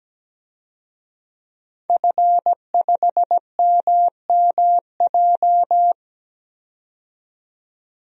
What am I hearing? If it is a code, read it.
Morse code, heard as a keyed tone, about 17 wpm, 700 Hz. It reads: F5MMJ
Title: DXCC-100 Call Signs - 5 Characters - 17wpm